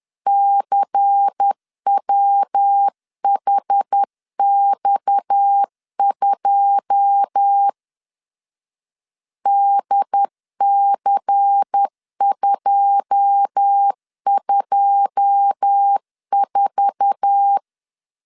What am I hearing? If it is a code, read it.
Morse code, heard as a keyed tone, about 10 wpm, 800 Hz. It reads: CWHX2 DC224
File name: LEZIONI / 9 2 10wpm 2736